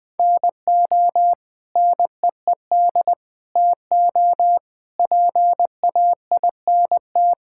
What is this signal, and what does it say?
Morse code, heard as a keyed tone, about 20 wpm, 700 Hz. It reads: NO NEED TO PAINT
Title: Sentences from Top 1000 Words - Review Part 6 - 20wpm